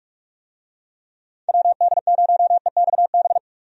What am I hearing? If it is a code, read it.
Morse code, heard as a keyed tone, about 46 wpm, 700 Hz. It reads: WD0EXB
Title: USA Call Signs - 6 Characters - Repeated 2x - 45wpm